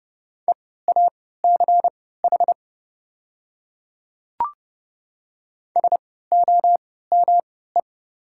Morse code, heard as a keyed tone, about 30 wpm, spaced as for 17 wpm, 700 Hz. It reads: EACH SOME
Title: 3x Character Spacing - Top 100 Words - 30wpm